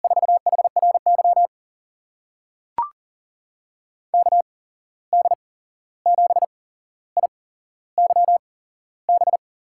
Morse code, heard as a keyed tone, about 40 wpm, spaced as for 15 wpm, 700 Hz. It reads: 4HRY KD7IYB
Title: USA Call Signs - 6 Characters - 8x Character Spacing - 40wpm